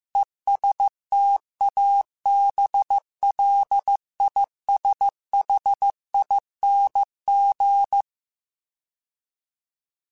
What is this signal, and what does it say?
Morse code, heard as a keyed tone, about 15 wpm, 800 Hz. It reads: ESTABLISHING
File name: Alphabetical / Alphabetical mp3s / Es @13